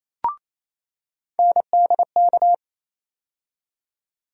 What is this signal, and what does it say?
Morse code, heard as a keyed tone, about 28 wpm, 700 Hz. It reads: NDK
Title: USA Call Sign Postfixes - 3 Letters 28wpm